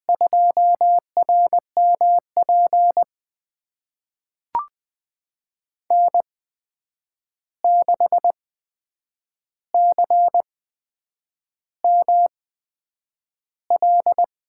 Morse code, heard as a keyed tone, about 20 wpm, spaced as for 8 wpm, 700 Hz. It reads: 2RMP N6CML
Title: USA Call Signs - 8x Character Spacing - 20wpm